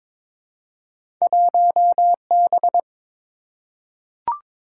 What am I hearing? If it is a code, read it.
Morse code, heard as a keyed tone, about 22 wpm, 700 Hz. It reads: 1B
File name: practice / 22wpm / Two Letter-Number Combinations - 22wpm